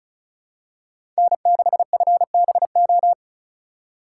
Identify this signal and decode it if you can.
Morse code, heard as a keyed tone, about 36 wpm, 690 Hz. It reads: N6FBO
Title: USA Call Signs - 5 Characters - 3x Character Spacing - 35wpm